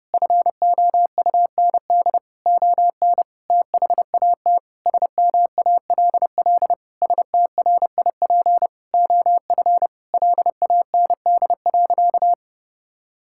Morse code, heard as a keyed tone, about 30 wpm, 700 Hz. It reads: FOUND ON THAT SMALL STRIP OF LAND.
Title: Sentences from Top 1600 Words 30wpm